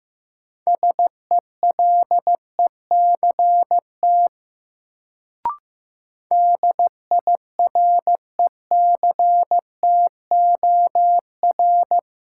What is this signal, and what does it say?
Morse code, heard as a keyed tone, about 15 wpm, 700 Hz. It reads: SELECT DIRECTOR
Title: Sentences from Top 1300 Words - New Word Drill - 15wpm